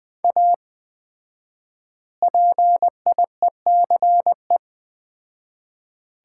Morse code, heard as a keyed tone, about 21 wpm, 700 Hz. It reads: A PIECE
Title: 4x Word-Spacing - Sets of 3 Words - 20wpm